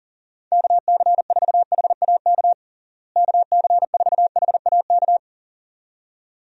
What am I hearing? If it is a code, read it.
Morse code, heard as a keyed tone, about 40 wpm, 700 Hz. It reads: KC4HAK KC4HAK
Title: USA Call Signs - 6 Characters - Repeated 2x - 40wpm